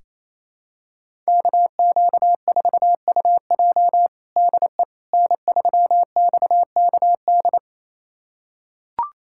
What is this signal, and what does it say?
Morse code, heard as a keyed tone, about 28 wpm, 700 Hz. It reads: KQ4UJ DE N3XKD
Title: USA Callsign DE Callsign - 5 Characters - 28wpm